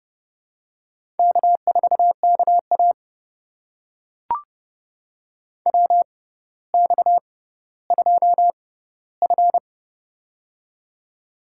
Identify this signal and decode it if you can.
Morse code, heard as a keyed tone, about 30 wpm, spaced as for 15 wpm, 700 Hz. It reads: K4KA WX2F
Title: USA Call Signs - 4 Characters - 6x Character Spacing - 30wpm